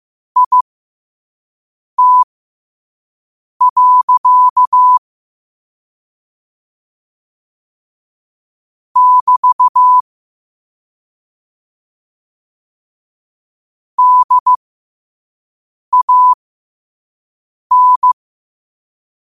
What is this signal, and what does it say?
Morse code, heard as a keyed tone, about 15 wpm, spaced as for 5 wpm, 1000 Hz. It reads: IT. <BT> DAN
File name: qotdfiles / 5wpm